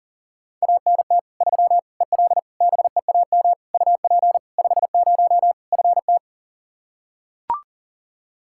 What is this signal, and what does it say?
Morse code, heard as a keyed tone, about 40 wpm, 700 Hz. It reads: ANT 3 EL BEAM UP 50 FT
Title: Practice QSO - Exchange 2 - Info - 40wpm